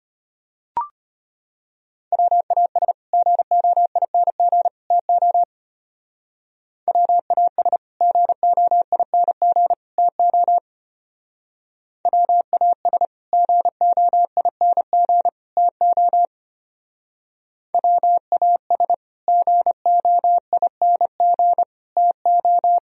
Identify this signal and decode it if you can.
Morse code, heard as a keyed tone, about 31 wpm, 700 Hz. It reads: WAS GOING TO WAS GOING TO WAS GOING TO WAS GOING TO
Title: Speed-Racing - Sets of 3 Words 25wpm